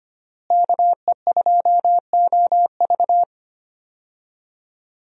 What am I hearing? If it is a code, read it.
Morse code, heard as a keyed tone, about 25 wpm, 700 Hz. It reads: KE2OV
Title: USA Call Signs - 5 Characters - Repeated 3x - 25wpm